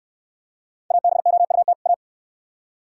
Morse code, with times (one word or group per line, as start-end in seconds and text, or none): 0.9-1.7 ABOUT
1.9-1.9 A